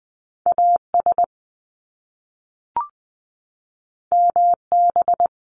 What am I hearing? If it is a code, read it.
Morse code, heard as a keyed tone, about 20 wpm, 700 Hz. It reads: A S M B
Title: Top Two Letter Combinations in Words 20wpm